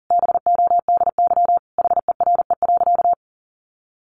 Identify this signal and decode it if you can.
Morse code, heard as a keyed tone, about 40 wpm, 700 Hz. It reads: BODY HERE.
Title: Sentences from Top 1100 Words 40wpm